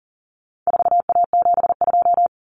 Morse code, heard as a keyed tone, about 40 wpm, 700 Hz. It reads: 4A72
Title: Common Cut Numbers - 4-Digit 40wpm